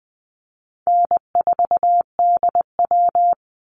Morse code, heard as a keyed tone, about 20 wpm, 700 Hz. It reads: N4DW
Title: Easy Call Signs Repeated 2x - 20wpm